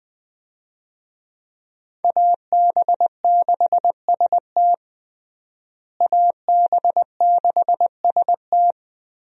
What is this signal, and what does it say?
Morse code, heard as a keyed tone, about 20 wpm, 700 Hz. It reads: AB6ST AB6ST
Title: USA Call Signs - 5 Characters - Repeated 3x - 20wpm